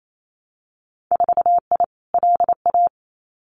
Morse code, heard as a keyed tone, about 28 wpm, 700 Hz. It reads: 4I LA
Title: Winter Field Day Exchanges Encore 28wpm